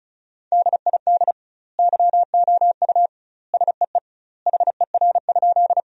Morse code, heard as a keyed tone, about 34 wpm, 700 Hz. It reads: DID YOU SEE HER?
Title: 2x Word-Spacing - Sentences from Top 100 Words - 35wpm